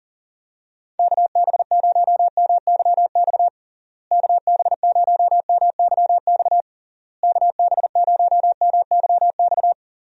Morse code, heard as a keyed tone, about 40 wpm, 700 Hz. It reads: KB0MYX KB0MYX KB0MYX
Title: USA Call Signs - 6 Characters - Repeated 3x - 40wpm